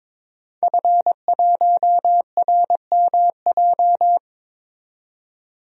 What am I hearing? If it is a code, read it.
Morse code, heard as a keyed tone, about 22 wpm, 700 Hz. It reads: F1RMJ
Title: DXCC-100 Call Signs - 5 Characters - 22wpm